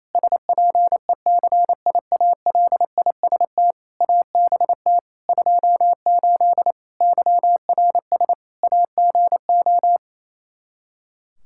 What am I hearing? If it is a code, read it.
Morse code, heard as a keyed tone, about 28 wpm, 700 Hz. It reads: SPECIALIST ABT 28 YRS AGO
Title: Practice QSO - Exchange 3 - More Info - 28wpm